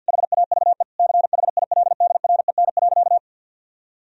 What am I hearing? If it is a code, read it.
Morse code, heard as a keyed tone, about 50 wpm, 700 Hz. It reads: HAVE CHILDREN.
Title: Sentences from Top 1000 Words - Review Part 1 - 50wpm